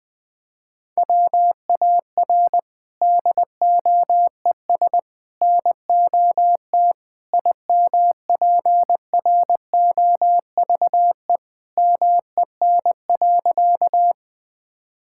Morse code, heard as a keyed tone, about 20 wpm, 700 Hz. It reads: WAR DOES NOT IMPROVE MEN.
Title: Sentences from Top 1000 Words - Review Part 5 - 20wpm